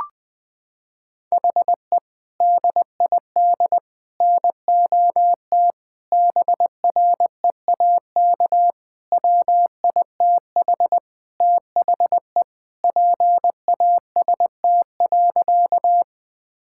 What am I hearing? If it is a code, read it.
Morse code, heard as a keyed tone, about 20 wpm, 700 Hz. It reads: HE DID NOT BREAK WITH THE PAST.
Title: Sentences from Top 1000 Words - Review Part 6 - 20wpm